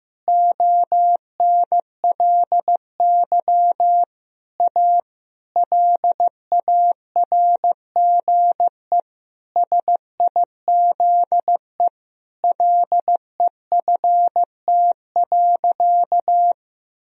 Morse code, heard as a keyed tone, about 15 wpm, 700 Hz. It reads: ONLY A LARGE SIZE LEFT.
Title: Sentences from Top 1000 Words - Review Part 2 - 15wpm